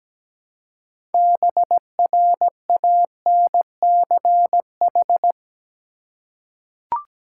Morse code, heard as a keyed tone, about 17 wpm, 700 Hz. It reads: BRANCH
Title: Six Letter Words - 17wpm